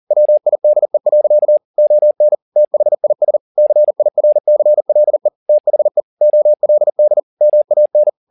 Morse code, heard as a keyed tone, about 40 wpm, 600 Hz. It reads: WIDE. ON THIS CIRCLE THE OLD MAN
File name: the old man and the sea 2 40wpm